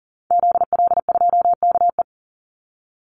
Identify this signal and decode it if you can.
Morse code, heard as a keyed tone, about 40 wpm, 700 Hz. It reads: ZL2KE
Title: Call Signs - 40wpm